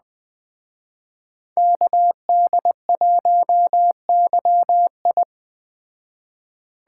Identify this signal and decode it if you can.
Morse code, heard as a keyed tone, about 19 wpm, 700 Hz. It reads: KD1YI